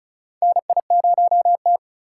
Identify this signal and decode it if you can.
Morse code, heard as a keyed tone, about 34 wpm, 700 Hz. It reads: NI0T
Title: USA Call Signs - 4x Character Spacing - 35wpm